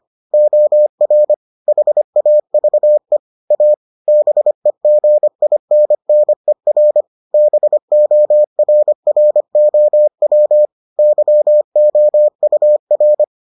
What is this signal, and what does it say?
Morse code, heard as a keyed tone, about 25 wpm, 600 Hz. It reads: OR HAVE A BEGINNER BORROW YOUR